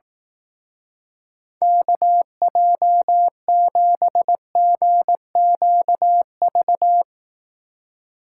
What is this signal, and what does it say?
Morse code, heard as a keyed tone, about 18 wpm, 700 Hz. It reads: KJ7GQV
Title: Speed Racing - USA Call Signs 15wpm